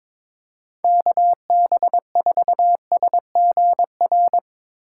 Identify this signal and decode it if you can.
Morse code, heard as a keyed tone, about 22 wpm, 700 Hz. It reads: KB4SGR
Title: USA Call Signs - 6 Characters - 3x Character Spacing - 22wpm